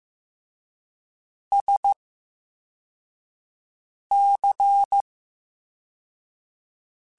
Morse code, heard as a keyed tone, about 15 wpm, spaced as for 4 wpm, 800 Hz. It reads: SC
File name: Special INGs / Special INGs-mp3s / Slow ing @13